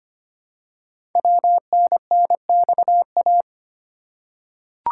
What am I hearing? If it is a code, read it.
Morse code, heard as a keyed tone, about 25 wpm, 700 Hz. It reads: WNNXA
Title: Code Groups - 5 Letters - 25wpm